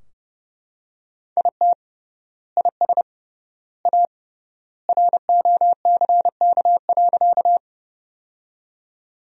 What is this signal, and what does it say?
Morse code, heard as a keyed tone, about 30 wpm, 700 Hz. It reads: IT IS A ROCK.
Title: 3x Word-Spacing - Sentences from Top 400 Words - 30wpm